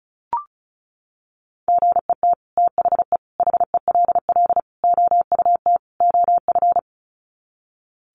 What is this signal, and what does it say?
Morse code, heard as a keyed tone, about 34 wpm, 700 Hz. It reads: GET THE HELL OUT OF
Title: Sets of 5 Words Encore2 35wpm